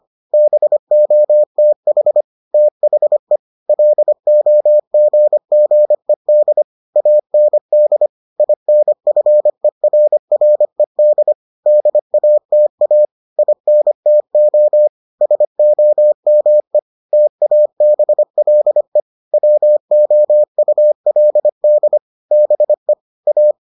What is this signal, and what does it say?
Morse code, heard as a keyed tone, about 25 wpm, 600 Hz. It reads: BOTH THE LOGGED AND INFERRED DATA INTO SOME TABLE WOULD BE A